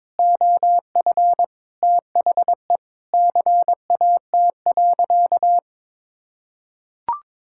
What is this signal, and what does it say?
Morse code, heard as a keyed tone, about 22 wpm, 700 Hz. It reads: OF THE CAT.
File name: practice / 22wpm / Sentences with 3-Word Phrases - 22wpm